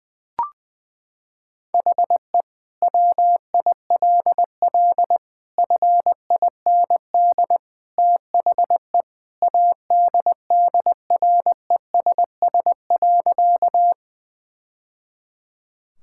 Morse code, heard as a keyed tone, about 20 wpm, 700 Hz. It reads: HE WILL FIND THE ADDRESS.
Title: Sentences from Top 1000 Words - Review Part 6 - 20wpm